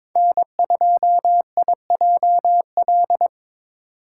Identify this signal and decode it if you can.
Morse code, heard as a keyed tone, about 21 wpm, 700 Hz. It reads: N2IJL
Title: USA Call Signs - 6x Character Spacing - 22wpm